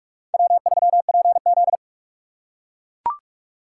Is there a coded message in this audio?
Yes: Morse code from 0.3 to 1.8 s, heard as a keyed tone, about 41 wpm, 700 Hz. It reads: W3PZ